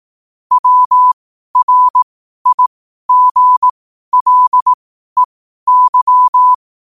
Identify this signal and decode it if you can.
Morse code, heard as a keyed tone, about 18 wpm, spaced as for 12 wpm, 1000 Hz. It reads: WRIGLEY
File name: qotdfiles / 12wpm